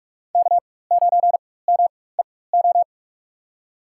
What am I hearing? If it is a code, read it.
Morse code, heard as a keyed tone, about 44 wpm, spaced as for 21 wpm, 700 Hz. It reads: K9MEO
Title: USA Call Signs - 4x Character Spacing - 45wpm